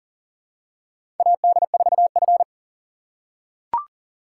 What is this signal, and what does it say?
Morse code, heard as a keyed tone, about 40 wpm, 700 Hz. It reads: AD4F
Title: USA Call Signs - 4x Character Spacing - 40wpm